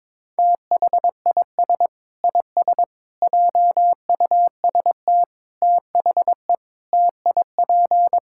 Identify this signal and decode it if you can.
Morse code, heard as a keyed tone, about 21 wpm, 700 Hz. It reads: THIS IS JUST THE TIP
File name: practice / 22wpm / Sentences with 8-Word Phrases - 22wpm